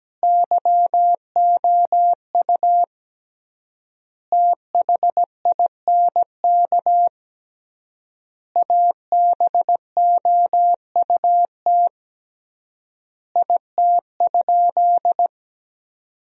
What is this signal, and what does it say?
Morse code, heard as a keyed tone, about 17 wpm, 700 Hz. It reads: YOU THINK ABOUT IT?
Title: 3x Word-Spacing - Sentences from Top 200 Words - 17wpm